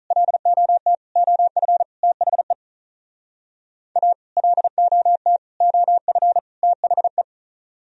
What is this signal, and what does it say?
Morse code, heard as a keyed tone, about 46 wpm, 700 Hz. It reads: LOT OF THE A LOT OF THE